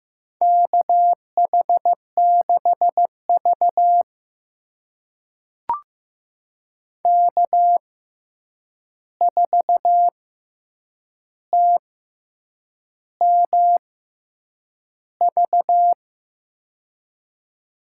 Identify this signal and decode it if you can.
Morse code, heard as a keyed tone, about 15 wpm, spaced as for 7 wpm, 700 Hz. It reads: KH6V K4TMV
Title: USA Call Signs - 6x Character Spacing - 15wpm